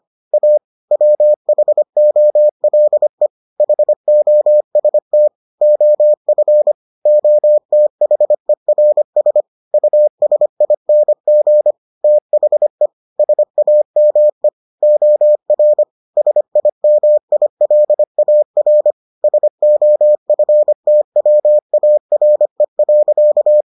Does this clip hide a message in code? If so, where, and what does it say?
Yes: Morse code from 0.3 to 23.6 s, heard as a keyed tone, about 25 wpm, 600 Hz. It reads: A WHOLE HOST OF OTHERS USING THE SAME OR SIMILAR SOFTWARE.